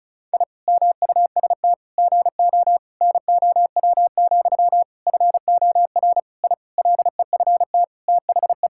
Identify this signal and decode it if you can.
Morse code, heard as a keyed tone, about 36 wpm, 700 Hz. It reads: I MUST GO NOW, FOR I LEFT THE
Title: Sentences from Top 1000 Words - Review Part 5 - 35wpm